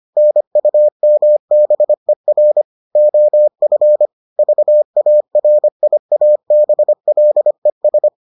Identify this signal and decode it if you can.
Morse code, heard as a keyed tone, about 25 wpm, 600 Hz. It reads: NUMBER OF VARIABLES